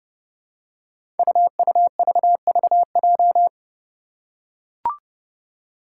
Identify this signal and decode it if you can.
Morse code, heard as a keyed tone, about 28 wpm, 700 Hz. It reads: UUVVJ
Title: Five Challenging Letter-Number Combinations 30wpm